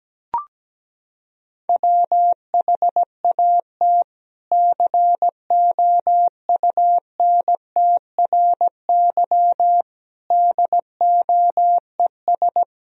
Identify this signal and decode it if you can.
Morse code, heard as a keyed tone, about 17 wpm, 700 Hz. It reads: WHAT COUNTRY DOES
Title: Sentences from Top 300 Words and Abbreviations 17wpm